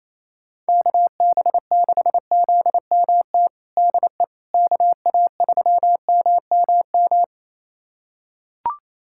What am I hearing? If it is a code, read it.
Morse code, heard as a keyed tone, about 28 wpm, 700 Hz. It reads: KB6ZMT DE KA3MMM
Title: USA Callsign DE Callsign 28wpm